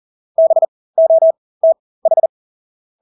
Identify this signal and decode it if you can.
Morse code, heard as a keyed tone, about 40 wpm, spaced as for 20 wpm, 650 Hz. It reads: BOTH
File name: Engelske ord 40-20 wpm